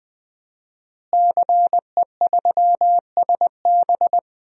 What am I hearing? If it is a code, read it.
Morse code, heard as a keyed tone, about 20 wpm, 700 Hz. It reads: CE3SB
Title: DXCC-100 Call Signs - 5 Characters - Repeated 2x - 20wpm